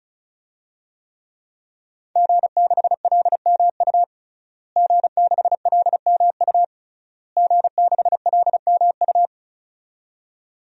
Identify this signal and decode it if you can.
Morse code, heard as a keyed tone, about 35 wpm, 700 Hz. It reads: G6LMU G6LMU G6LMU
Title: DXCC-100 Call Signs - 5 Characters - Repeated 3x - 35wpm